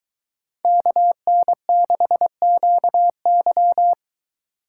Morse code, heard as a keyed tone, about 23 wpm, 700 Hz. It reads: KN6QY